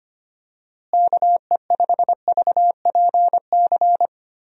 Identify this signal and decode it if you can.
Morse code, heard as a keyed tone, about 25 wpm, 700 Hz. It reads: KE5VPC
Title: USA Call Signs - 6 Characters - Repeated 3x - 25wpm